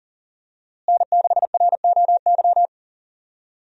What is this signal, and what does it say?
Morse code, heard as a keyed tone, about 40 wpm, 700 Hz. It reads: N6ROY
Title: USA Call Signs - 5 Characters - Repeated 3x - 40wpm